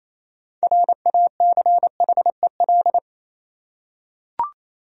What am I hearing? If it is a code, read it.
Morse code, heard as a keyed tone, about 28 wpm, 700 Hz. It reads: RACHEL
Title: USA Common Names - 28wpm